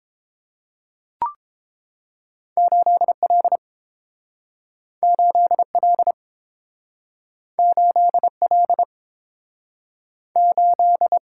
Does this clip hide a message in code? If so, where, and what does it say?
Yes: Morse code from 2.6 to 11.2 s, heard as a keyed tone, about 29 wpm, 700 Hz. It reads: 8L 8L 8L 8